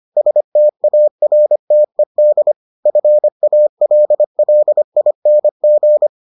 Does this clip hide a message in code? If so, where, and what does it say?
Yes: Morse code from 0.2 to 6.1 s, heard as a keyed tone, about 25 wpm, 600 Hz. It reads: STARTED FALLING